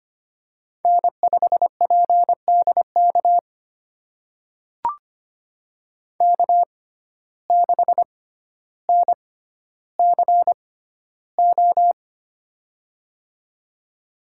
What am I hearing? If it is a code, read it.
Morse code, heard as a keyed tone, about 25 wpm, spaced as for 13 wpm, 700 Hz. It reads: N5PDK K6NCO
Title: USA Call Signs - 5 Characters - 6x Character Spacing - 25wpm